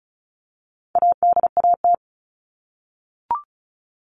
Morse code, heard as a keyed tone, about 34 wpm, 700 Hz. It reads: ADAT
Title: All Cut Numbers - 4-Digit 35wpm